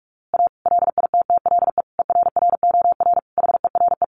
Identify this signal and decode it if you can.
Morse code, heard as a keyed tone, about 44 wpm, 700 Hz. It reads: A LITTLE ERROR HERE
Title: A Sound Of Thunder 45wpm